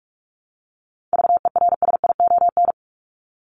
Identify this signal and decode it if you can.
Morse code, heard as a keyed tone, about 44 wpm, 700 Hz. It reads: VERSION
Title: Top 1251-1500 Words 45wpm